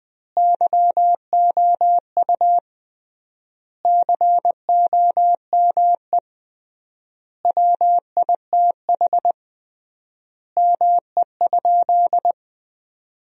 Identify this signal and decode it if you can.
Morse code, heard as a keyed tone, about 21 wpm, 700 Hz. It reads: YOU COME WITH ME?
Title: 3x Word-Spacing - Sentences from Top 200 Words - 20wpm